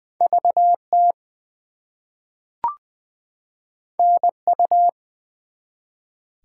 Morse code, heard as a keyed tone, about 20 wpm, 700 Hz. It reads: VT NU
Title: US States and Canadian Provinces 20wpm